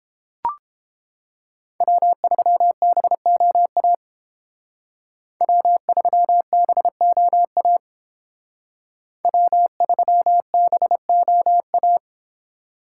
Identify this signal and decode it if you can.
Morse code, heard as a keyed tone, about 30 wpm, 700 Hz. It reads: W3BOA W3BOA W3BOA